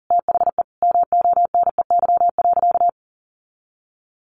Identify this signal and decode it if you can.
Morse code, heard as a keyed tone, about 40 wpm, 700 Hz. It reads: THE MONEY.
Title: Sentences from Top 600 Words 40wpm